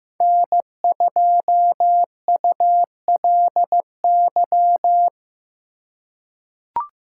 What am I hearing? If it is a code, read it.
Morse code, heard as a keyed tone, about 15 wpm, 700 Hz. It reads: N2ULY
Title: USA Call Signs - 5 Characters - 2x Character Spacing - 15wpm